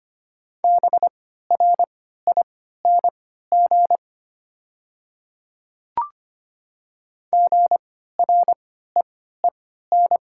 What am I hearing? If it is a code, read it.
Morse code, heard as a keyed tone, about 25 wpm, spaced as for 14 wpm, 700 Hz. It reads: BRING GREEN
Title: 3x Character Spacing - Top 500 Words - 25wpm